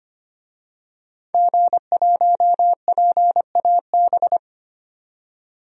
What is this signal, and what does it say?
Morse code, heard as a keyed tone, about 25 wpm, 700 Hz. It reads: G1PAB
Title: DXCC-100 Call Signs - 5 Characters - 25wpm